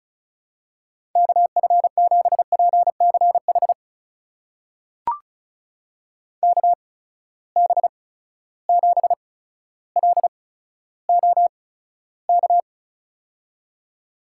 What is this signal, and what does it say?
Morse code, heard as a keyed tone, about 36 wpm, spaced as for 15 wpm, 700 Hz. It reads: KF7PCH KB7LOK
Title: USA Call Signs - 6 Characters - 8x Character Spacing - 35wpm